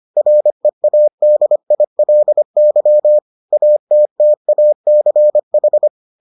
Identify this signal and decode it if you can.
Morse code, heard as a keyed tone, about 25 wpm, 600 Hz. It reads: READILY ATTACH